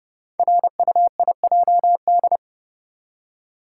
Morse code, heard as a keyed tone, about 30 wpm, 700 Hz. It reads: RUIJD
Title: Code Groups - 5 Letters - 30wpm